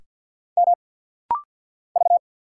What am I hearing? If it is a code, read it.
Morse code, heard as a keyed tone, about 50 wpm, 690 Hz. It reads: M V
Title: Single Letter-Number Alt - Rapid-Fire 50wpm